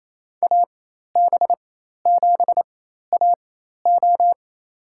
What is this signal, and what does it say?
Morse code, heard as a keyed tone, about 28 wpm, spaced as for 13 wpm, 700 Hz. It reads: AB7AO
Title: USA Call Signs - 5 Characters - 4x Character Spacing - 28wpm